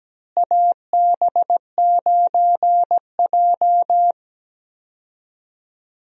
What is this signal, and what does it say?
Morse code, heard as a keyed tone, about 17 wpm, 700 Hz. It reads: AB9J
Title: USA Call Signs - 4 Characters - Repeated 2x - 17wpm